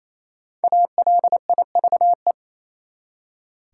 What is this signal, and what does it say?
Morse code, heard as a keyed tone, about 28 wpm, 700 Hz. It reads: ALIVE
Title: Five Letter Words - 28wpm